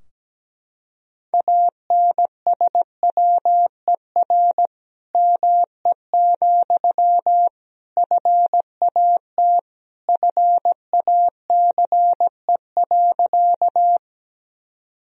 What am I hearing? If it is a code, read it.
Morse code, heard as a keyed tone, about 17 wpm, 700 Hz. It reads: ANSWER ME, FAT FACE.
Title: Sentences from Top 1000 Words - Review Part 4 - 17wpm